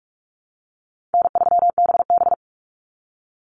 Morse code, heard as a keyed tone, about 44 wpm, 700 Hz. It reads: N3BB
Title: Call Signs Repeated 3x - 45wpm